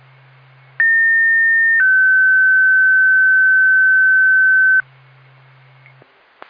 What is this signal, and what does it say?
Two-tone page, heard as a one-second tone, then a three-second tone, about 1800 then 1550 Hz. It is heard from 0.8 to 4.8 s.